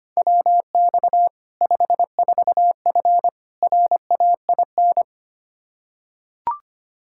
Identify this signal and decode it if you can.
Morse code, heard as a keyed tone, about 25 wpm, 700 Hz. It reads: WX 54F RAIN
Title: Practice QSO - Exchange 2 - Info - 25wpm